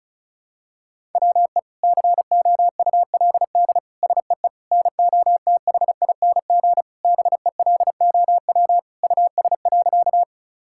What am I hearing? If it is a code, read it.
Morse code, heard as a keyed tone, about 34 wpm, 700 Hz. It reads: WE COULD SEE NOTHING BELOW US.
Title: Sentences from Top 1000 Words - Review Part 5 - 35wpm